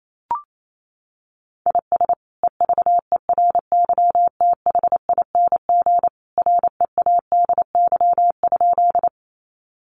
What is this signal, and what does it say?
Morse code, heard as a keyed tone, about 28 wpm, 700 Hz. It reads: IS EVERYTHING READY?
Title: Sentences from Top 700 Words 28wpm